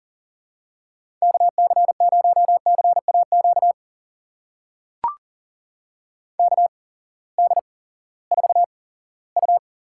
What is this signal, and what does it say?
Morse code, heard as a keyed tone, about 39 wpm, spaced as for 20 wpm, 700 Hz. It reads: KC0CAQ KD4U
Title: USA Call Signs - 6 Characters - 8x Character Spacing - 40wpm